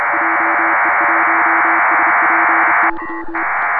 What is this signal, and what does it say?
Morse code, heard as a keyed tone, about 31 wpm, 330 Hz, with noise. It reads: JE1SPY
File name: Pile-up with PACTOR3